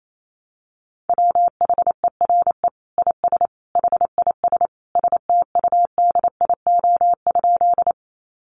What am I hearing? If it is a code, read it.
Morse code, heard as a keyed tone, about 28 wpm, 700 Hz. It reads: WHERE IS HIS STUDIO?
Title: Sentences from Top 1200 Words 28wpm